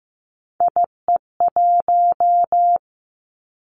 Morse code, heard as a keyed tone, about 15 wpm, 700 Hz. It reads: IE1
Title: Three Challenging Letter-Number Combinations 15wpm